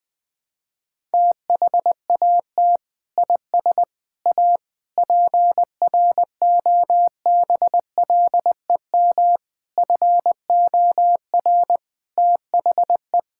Morse code, heard as a keyed tone, about 20 wpm, 700 Hz. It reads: THAT IS A PROBLEM FOR THE
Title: Sentences from Top 1600 Words 20wpm